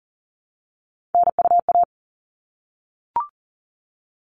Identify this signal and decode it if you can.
Morse code, heard as a keyed tone, about 40 wpm, 700 Hz. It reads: NUA
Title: All Cut Numbers - 3-Digit 40wpm